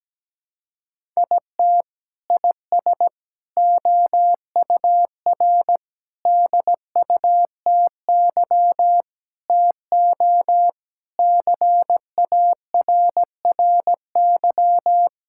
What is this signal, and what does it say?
Morse code, heard as a keyed tone, about 17 wpm, 700 Hz. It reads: IT IS OUR DUTY TO CARRY
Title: Sentences from Top 1600 Words 17wpm